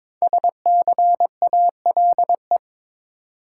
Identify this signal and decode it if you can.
Morse code, heard as a keyed tone, about 23 wpm, 700 Hz. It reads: SCALE
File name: 5 Letter Words - First 3 Letters Given - 22wpm